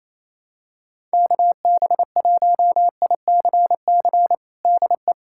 Morse code, heard as a keyed tone, about 28 wpm, 700 Hz. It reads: KB1ICC DE
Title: USA Callsign DE Callsign 28wpm